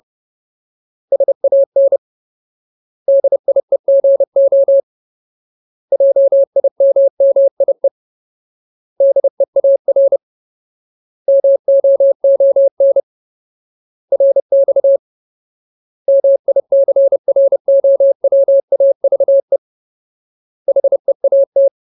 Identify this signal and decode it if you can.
Morse code, heard as a keyed tone, about 30 wpm, 550 Hz. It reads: SAN DIEGO JIMMIE DEAR MOON RX MICROWAVE HEAT
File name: Ultimate Word List / 30WPM Ultimate